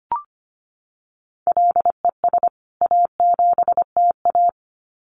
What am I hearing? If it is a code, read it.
Morse code, heard as a keyed tone, about 25 wpm, 700 Hz. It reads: LES A7TA
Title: CWT Exchange 25wpm